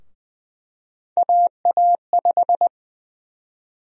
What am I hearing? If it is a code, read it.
Morse code, heard as a keyed tone, about 21 wpm, 700 Hz. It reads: AA5
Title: USA Call Sign Prefixes - Repeated 2x - 20wpm